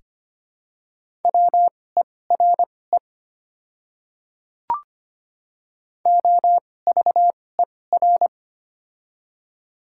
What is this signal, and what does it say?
Morse code, heard as a keyed tone, about 25 wpm, spaced as for 18 wpm, 700 Hz. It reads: WERE OVER
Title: 2x Character Spacing - Top 100 Words - 25wpm